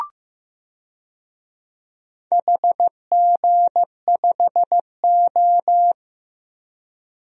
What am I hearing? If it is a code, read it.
Morse code, heard as a keyed tone, about 15 wpm, 700 Hz. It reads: HG5O
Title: DXCC-100 Call Signs - 4 Characters - Repeated 2x - 15wpm